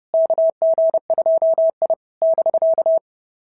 Morse code, heard as a keyed tone, about 30 wpm, 650 Hz. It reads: KG2I <BK>
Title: SOTA QSO Practice 30wpm